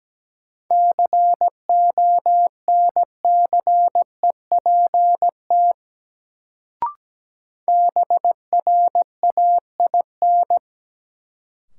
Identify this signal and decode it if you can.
Morse code, heard as a keyed tone, about 17 wpm, 700 Hz. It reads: CONCEPT BRAIN
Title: Sentences from Top 1100 Words - New Word Drill Alt - 17wpm